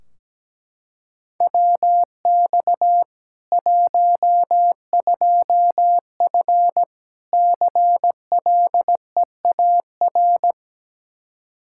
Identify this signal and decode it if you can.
Morse code, heard as a keyed tone, about 17 wpm, 700 Hz. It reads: WX 12F CLEAR
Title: Practice QSO - Exchange 2 - Info - 17wpm